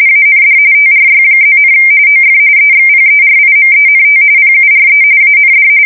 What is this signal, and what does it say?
RTTY, about 45.45 baud. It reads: AND TRY+, OF EXPERIMENT PURE AND